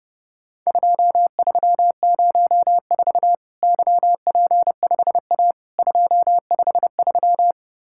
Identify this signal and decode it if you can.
Morse code, heard as a keyed tone, about 29 wpm, 700 Hz. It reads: 2304 YP5A 253
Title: Worked All Europe DX Contest - QTC Exchange - 30wpm